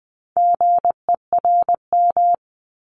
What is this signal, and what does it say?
Morse code, heard as a keyed tone, about 20 wpm, 700 Hz. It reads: GERM
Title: Multi-letter Phonemes 20wpm